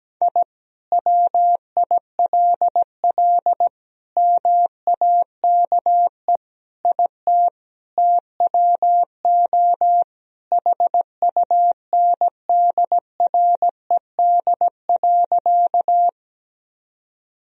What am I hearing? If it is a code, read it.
Morse code, heard as a keyed tone, about 17 wpm, 700 Hz. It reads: I WILL MAKE IT TWO HUNDRED.
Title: Sentences from Top 1000 Words - Review Part 2 - 17wpm